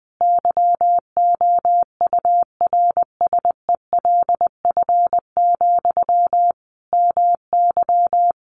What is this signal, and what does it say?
Morse code, heard as a keyed tone, about 20 wpm, 700 Hz. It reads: YOURSELF, MY
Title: Sentences from Top 1100 Words 20wpm